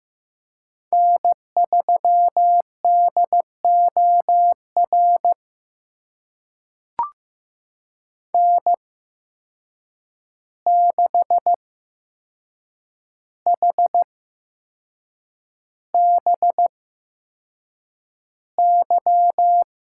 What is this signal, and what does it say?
Morse code, heard as a keyed tone, about 15 wpm, spaced as for 6 wpm, 700 Hz. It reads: N3DOR N6HBY
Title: USA Call Signs - 5 Characters - 8x Character Spacing - 15wpm